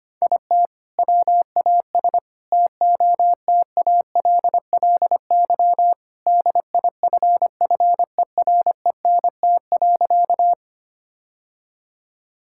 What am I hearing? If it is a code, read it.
Morse code, heard as a keyed tone, about 25 wpm, 700 Hz. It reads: IT WAS TOTALLY DIFFERENT.
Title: Sentences from Top 1600 Words 25wpm